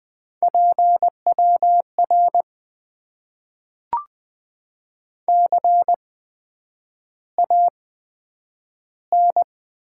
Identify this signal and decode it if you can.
Morse code, heard as a keyed tone, about 20 wpm, spaced as for 9 wpm, 700 Hz. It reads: PWR CAN
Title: Top 90 QSO Elements - 8x Character Spacing - 20wpm